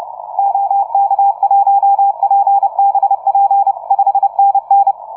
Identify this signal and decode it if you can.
Morse code, heard as a keyed tone, about 30 wpm, 790 Hz. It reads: OK1PBP 5NN
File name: on6jun / OZ ON6JUN-CW-30m